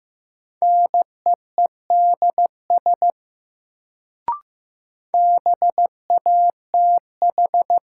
Morse code, heard as a keyed tone, about 15 wpm, 700 Hz. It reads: NEEDS BATH
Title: Sentences from Top 1600 Words - New Word Drill Alt - 15wpm